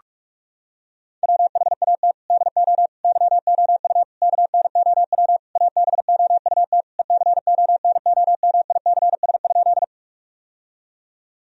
Morse code, heard as a keyed tone, about 46 wpm, 700 Hz. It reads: WHAT DO YOU KNOW ABOUT ECONOMICS?
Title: Sentences from Top 1000 Words - Review Part 3 - 45wpm